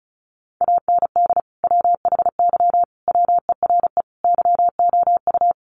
Morse code, heard as a keyed tone, about 34 wpm, 700 Hz. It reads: AND WHY WERE YOU
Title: Sentences from Top 1000 Words 35wpm